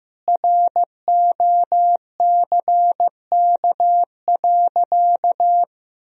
Morse code, heard as a keyed tone, about 15 wpm, 700 Hz. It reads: ROCK.
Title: Sentences from Top 1000 Words - Review Part 2 - 15wpm